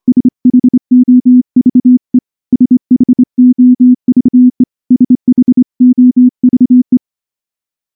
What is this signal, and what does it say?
Morse code, heard as a keyed tone, about 24 wpm, 250 Hz. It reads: SHOVE SHOVE SHOVE
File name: morse-training-25WPM